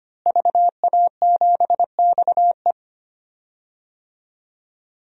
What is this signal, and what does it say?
Morse code, heard as a keyed tone, about 25 wpm, 700 Hz. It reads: VA7XE
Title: DXCC-100 Call Signs - 5 Characters - Repeated 3x - 25wpm